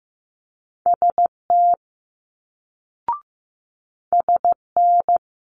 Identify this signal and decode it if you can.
Morse code, heard as a keyed tone, about 15 wpm, 700 Hz. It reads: ST SN
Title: Multi-letter Phonemes Quick Fire 15wpm